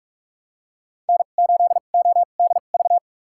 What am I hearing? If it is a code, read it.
Morse code, heard as a keyed tone, about 48 wpm, spaced as for 33 wpm, 690 Hz. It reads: N8ODV
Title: USA Call Signs - 5 Characters - 2x Character Spacing - 45wpm